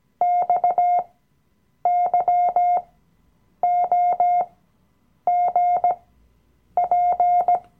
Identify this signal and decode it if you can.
Morse code, heard as a keyed tone, about 18 wpm, spaced as for 8 wpm, 710 Hz. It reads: XYOGP
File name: Workshop Dateien / CW-Online-Kurs / Lektion 007